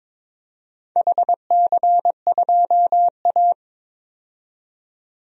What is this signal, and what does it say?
Morse code, heard as a keyed tone, about 22 wpm, 690 Hz. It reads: HC2A
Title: DXCC-100 Call Signs - 4 Characters - 22wpm